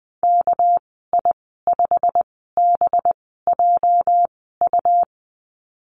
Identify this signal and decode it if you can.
Morse code, heard as a keyed tone, about 20 wpm, spaced as for 14 wpm, 700 Hz. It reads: KI5BJU
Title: 219 Lesson 35g - Number 0 - Call Signs 20wpm